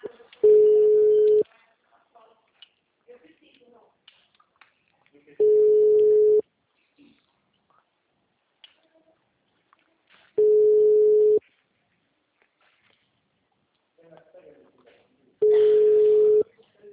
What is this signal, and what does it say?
Ringback tone (Europe, 425 Hz), with speech.
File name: out- -2014- - - .40279